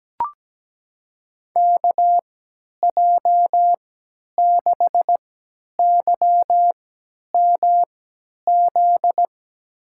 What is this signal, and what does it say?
Morse code, heard as a keyed tone, about 17 wpm, spaced as for 10 wpm, 700 Hz. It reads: KJ6YMZ
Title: USA Call Signs - 6 Characters - 3x Character Spacing - 17wpm